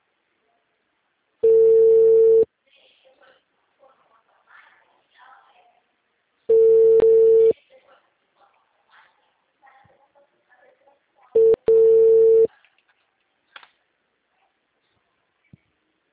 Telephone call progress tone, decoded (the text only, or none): ringback tone